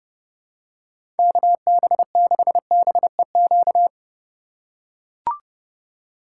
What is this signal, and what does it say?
Morse code, heard as a keyed tone, about 32 wpm, 700 Hz. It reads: KB6BEQ
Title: USA Call Signs - 6 Characters - 2x Character Spacing - 30wpm